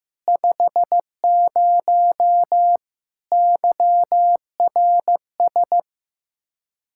Morse code, heard as a keyed tone, about 15 wpm, 700 Hz. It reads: 50 YRS
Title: K1USN SST QSO Practice 15wpm